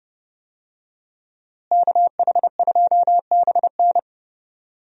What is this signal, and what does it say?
Morse code, heard as a keyed tone, about 32 wpm, 700 Hz. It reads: KH2BN
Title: USA Call Signs - 5 Characters - Repeated 2x - 30wpm